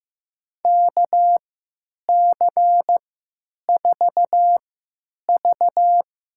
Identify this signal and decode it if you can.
Morse code, heard as a keyed tone, about 15 wpm, spaced as for 9 wpm, 700 Hz. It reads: KC4V
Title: USA Call Signs - 3x Character Spacing - 15wpm